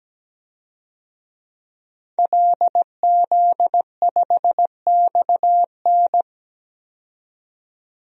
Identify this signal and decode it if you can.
Morse code, heard as a keyed tone, about 17 wpm, 700 Hz. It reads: LZ5XN
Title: DXCC-100 Call Signs - 5 Characters - 17wpm